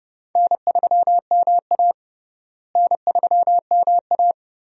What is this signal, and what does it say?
Morse code, heard as a keyed tone, about 32 wpm, 700 Hz. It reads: N3MA N3MA
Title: USA Call Signs - 4 Characters - Repeated 3x - 30wpm